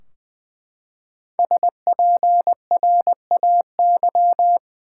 Morse code, heard as a keyed tone, about 21 wpm, 700 Hz. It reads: SPRAY
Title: Five Letter Words - 20wpm